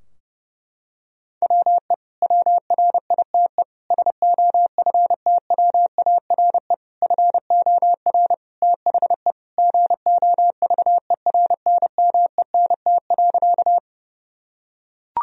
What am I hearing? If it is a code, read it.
Morse code, heard as a keyed tone, about 32 wpm, 700 Hz. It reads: WE WRITE SOFTWARE FOR THE GOVERNMENT.
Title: Sentences from Top 1000 Words - Review Part 3 - 30wpm